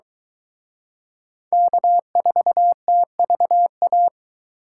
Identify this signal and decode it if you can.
Morse code, heard as a keyed tone, about 23 wpm, 700 Hz. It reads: K4TVA